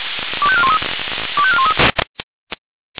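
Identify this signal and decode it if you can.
Five-tone selective call (ZVEI), heard as quick runs of five tones, tones of about 70 ms. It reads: 25413 25413
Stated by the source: ZVEI five-tone selective call received on an SDR.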